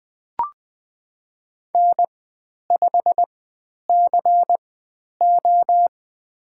Morse code, heard as a keyed tone, about 20 wpm, spaced as for 10 wpm, 700 Hz. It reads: N5CO
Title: 095 Lesson 16g - Letter c - Call Signs 20wpm